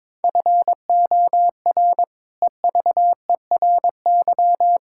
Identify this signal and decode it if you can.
Morse code, heard as a keyed tone, about 22 wpm, 700 Hz. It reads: FOR EVERY
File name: practice / 22wpm / Sets of 8 Words - 22wpm